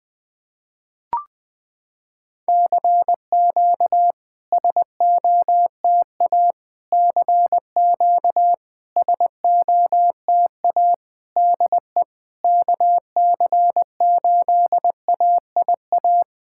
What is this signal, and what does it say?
Morse code, heard as a keyed tone, about 20 wpm, 700 Hz. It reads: CQ SOTA CQ SOTA DE KC8AIA
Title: SOTA QSO Practice 20wpm